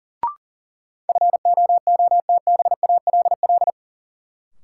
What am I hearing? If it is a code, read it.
Morse code, heard as a keyed tone, about 37 wpm, 700 Hz. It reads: FOOTBALL
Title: Sentences from Top 1600 Words - New Word Drill Alt - 40wpm